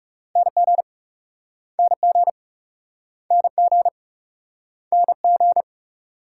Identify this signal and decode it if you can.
Morse code, heard as a keyed tone, about 40 wpm, 700 Hz. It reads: NG NG NG NG